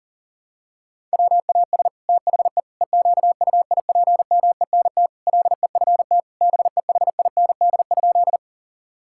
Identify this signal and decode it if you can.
Morse code, heard as a keyed tone, about 40 wpm, 700 Hz. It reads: WAS THE EQUIPMENT LEFT BEHIND?
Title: Sentences from Top 1000 Words - Review Part 5 - 40wpm